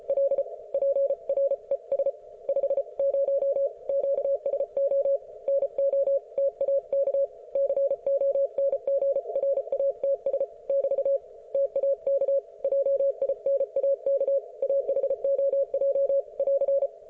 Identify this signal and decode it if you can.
Morse code, heard as a keyed tone, about 34 wpm, 550 Hz. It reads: L PRES 50 QSO NO TAK CONGRATS <BT> TAK JINAK AHOJ <AR>